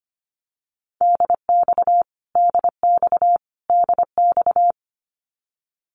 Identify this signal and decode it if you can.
Morse code, heard as a keyed tone, about 25 wpm, 700 Hz. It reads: DX DX DX
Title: Abbreviations Repeated 3x - 25wpm